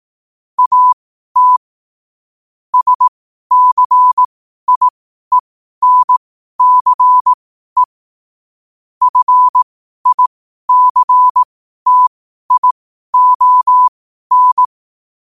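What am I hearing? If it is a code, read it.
Morse code, heard as a keyed tone, about 18 wpm, spaced as for 12 wpm, 1000 Hz. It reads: AT SCIENCE FICTION
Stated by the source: Morse code quote of the day at 12 wpm.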